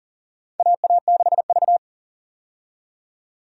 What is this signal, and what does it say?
Morse code, heard as a keyed tone, about 40 wpm, 700 Hz. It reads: AA6V
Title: USA Call Signs - 4 Characters - 40wpm